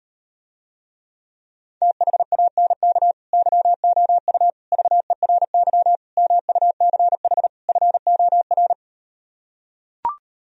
Morse code, heard as a keyed tone, about 38 wpm, 700 Hz. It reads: THANK YOU VERY MUCH FOR